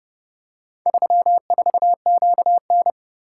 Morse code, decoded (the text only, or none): 34QN